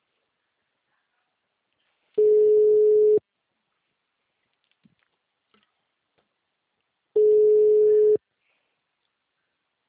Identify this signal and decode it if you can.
Ringback tone (Europe, 425 Hz).
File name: out- -2015- - - .44510